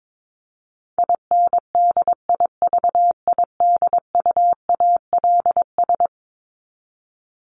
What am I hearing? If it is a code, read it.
Morse code, heard as a keyed tone, about 22 wpm, 700 Hz. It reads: INDIVIDUALS
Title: ES:S Suffix - Top 100 Words with Base Words 22wpm